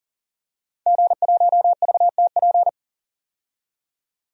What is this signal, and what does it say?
Morse code, heard as a keyed tone, about 38 wpm, 700 Hz. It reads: G1VTP
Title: DXCC-100 Call Signs - 5 Characters - 40wpm